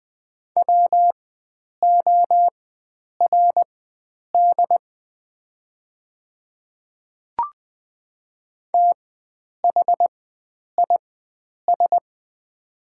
Morse code, heard as a keyed tone, about 21 wpm, spaced as for 9 wpm, 690 Hz. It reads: WORD THIS
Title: 4x Character Spacing - Top 100 Words - 20wpm